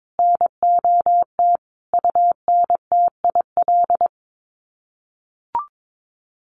Morse code, heard as a keyed tone, about 22 wpm, 700 Hz. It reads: NOT UNTIL
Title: Sets of 5 Words Encore2 22wpm